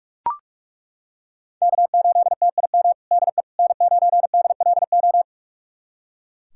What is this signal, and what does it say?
Morse code, heard as a keyed tone, about 41 wpm, 700 Hz. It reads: K8TIM DE N9DLO
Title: USA Callsign DE Callsign - 5 Characters - 45wpm